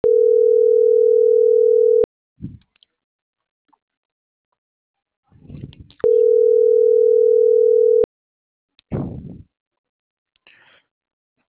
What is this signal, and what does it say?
Ringback tone (North America), 440 + 480 Hz.